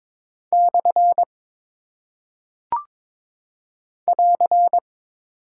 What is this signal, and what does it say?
Morse code, heard as a keyed tone, about 22 wpm, 700 Hz. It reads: / <AR>
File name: practice / 22wpm / Single Punctuation-Prosign - 22wpm